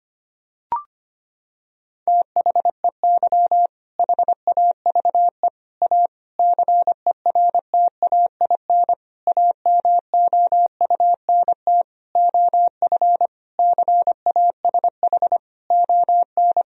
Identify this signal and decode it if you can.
Morse code, heard as a keyed tone, about 25 wpm, 700 Hz. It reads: THEY HAVE A CERTAIN AMOUNT OF CASH ON
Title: Sentences from Top 1000 Words - Review Part 4 - 25wpm